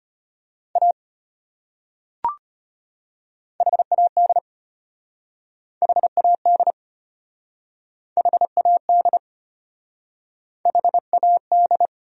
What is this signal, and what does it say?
Morse code, heard as a keyed tone, about 33 wpm, 700 Hz. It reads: A HAD HAD HAD HAD